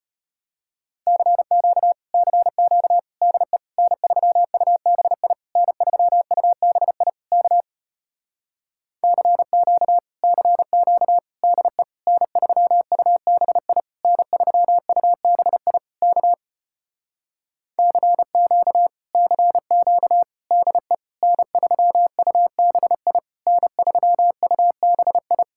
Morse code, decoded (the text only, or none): CQ CQ DE N3UBI N3UBI K CQ CQ DE N3UBI N3UBI K CQ CQ DE N3UBI N3UBI